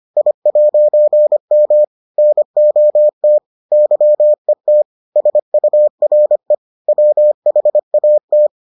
Morse code, heard as a keyed tone, about 25 wpm, 600 Hz. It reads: I'M NOT YET SURE WHAT